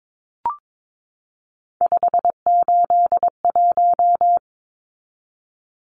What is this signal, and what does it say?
Morse code, heard as a keyed tone, about 22 wpm, 700 Hz. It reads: 581
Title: Numbers - 3-Digit 22wpm